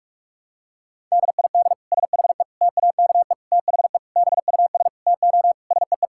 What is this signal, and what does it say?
Morse code, heard as a keyed tone, about 46 wpm, 700 Hz. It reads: DID SHE TAKE THE BUS TO SEE
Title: Sentences from Top 1000 Words - Review Part 6 - 45wpm